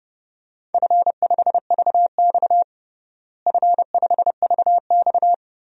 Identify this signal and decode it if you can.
Morse code, heard as a keyed tone, about 30 wpm, 700 Hz. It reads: F5VX F5VX
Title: DXCC-100 Call Signs - 4 Characters - Repeated 3x - 30wpm